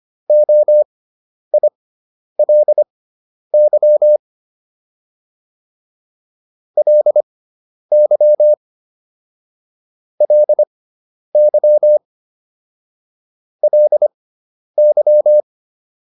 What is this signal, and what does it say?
Morse code, heard as a keyed tone, about 25 wpm, spaced as for 10 wpm, 600 Hz. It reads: OILY LY LY LY